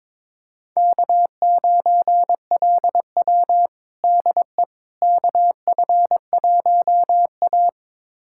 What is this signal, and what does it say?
Morse code, heard as a keyed tone, about 22 wpm, 700 Hz. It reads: K9LW DE KF1A
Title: USA Callsign DE Callsign - 4 Characters - 22wpm